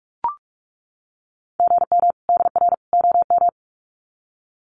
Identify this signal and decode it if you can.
Morse code, heard as a keyed tone, about 44 wpm, 700 Hz. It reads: GM DR OM
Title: Common QSO Phrases 45wpm